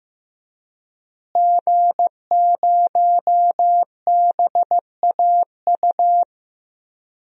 Morse code, heard as a keyed tone, about 15 wpm, 700 Hz. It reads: G0BAU
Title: DXCC-100 Call Signs - 5 Characters - Repeated 3x - 15wpm